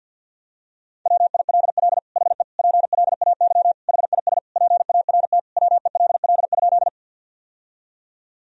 Morse code, heard as a keyed tone, about 48 wpm, 700 Hz. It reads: WILL HE PLAY HIS PART WELL?
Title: Sentences from Top 1000 Words - Review Part 1 - 50wpm